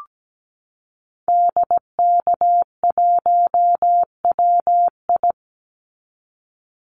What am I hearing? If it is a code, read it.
Morse code, heard as a keyed tone, about 17 wpm, 700 Hz. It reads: DK1WI
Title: Call Signs Repeated 2x - 17wpm